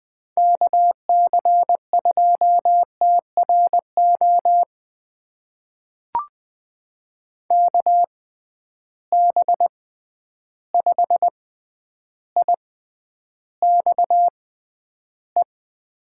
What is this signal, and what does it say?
Morse code, heard as a keyed tone, about 20 wpm, spaced as for 10 wpm, 700 Hz. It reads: KC2TRO KB5IXE
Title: USA Call Signs - 6 Characters - 6x Character Spacing - 20wpm